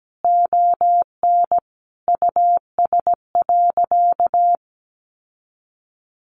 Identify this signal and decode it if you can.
Morse code, heard as a keyed tone, about 17 wpm, 700 Hz. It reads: ON US.
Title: Sentences from Top 1300 Words 17wpm